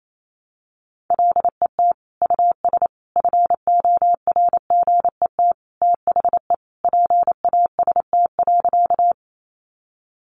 Morse code, heard as a keyed tone, about 28 wpm, 700 Hz. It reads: LET US FORGET THE PAST.
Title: Sentences from Top 1200 Words 28wpm